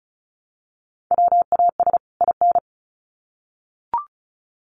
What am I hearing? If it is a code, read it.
Morse code, heard as a keyed tone, about 34 wpm, 700 Hz. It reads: WAS IN
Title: Sets of 2 Words 35wpm